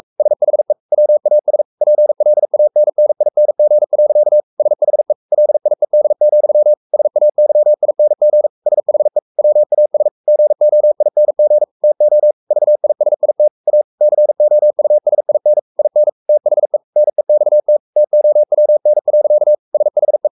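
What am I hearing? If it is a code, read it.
Morse code, heard as a keyed tone, about 44 wpm, 600 Hz. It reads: SHE WAS PLANNING. SHE LIED, SAYING SHE WAS GOING TO VISIT A COUSIN IN THE NEXT TOWN. SHE
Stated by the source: Morse code at 43 wpm sending Ken Follett's novel Never in English, from CW med Gnister.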